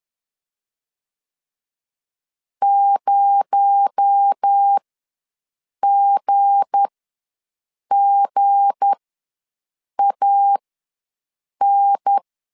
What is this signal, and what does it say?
Morse code, heard as a keyed tone, about 10 wpm, spaced as for 6 wpm, 800 Hz. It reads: 0GGAN
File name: LEZIONI / 7 GF9Z / 7 G ws 10wpm 3843